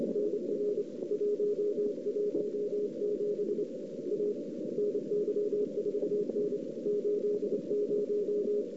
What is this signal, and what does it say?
Morse code, heard as a keyed tone, about 30 wpm, 410 Hz, with noise. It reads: NR 117 A NY4N 80
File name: CW / 2008-ss-cw-k5tr-op-n5rz -04:00